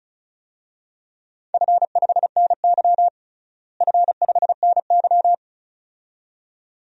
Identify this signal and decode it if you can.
Morse code, heard as a keyed tone, about 34 wpm, 700 Hz. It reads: F5NY F5NY
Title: DXCC-100 Call Signs - 4 Characters - Repeated 2x - 35wpm